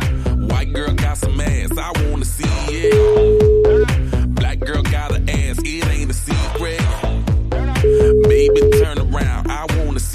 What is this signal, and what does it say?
Ringback tone (Europe, 425 Hz), with noise.